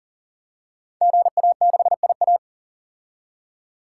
Morse code, heard as a keyed tone, about 40 wpm, 700 Hz. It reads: GA6IA
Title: Code Groups - 5 Letter:Numbers - 40wpm